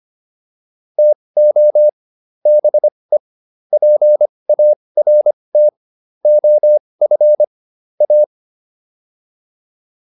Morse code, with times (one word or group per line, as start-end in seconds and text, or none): 1.0-1.9 TO
2.5-3.2 BE
3.7-5.7 PART
6.3-7.4 OF
8.0-8.2 A